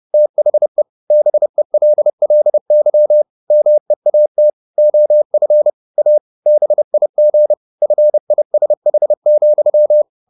Morse code, heard as a keyed tone, about 30 wpm, 600 Hz. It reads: THE BELLY MEAT OF A BIG FISH,
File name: the old man and the sea / the old man and the sea 1 30wpm